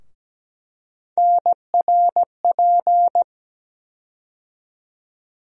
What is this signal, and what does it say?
Morse code, heard as a keyed tone, about 17 wpm, 700 Hz. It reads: NRP
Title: USA Call Sign Postfixes - 3 Letters 17wpm